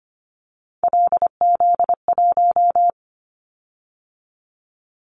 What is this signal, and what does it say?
Morse code, heard as a keyed tone, about 25 wpm, 700 Hz. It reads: LZ1
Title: Callsign Prefixes for Top 50 DXCC - 25wpm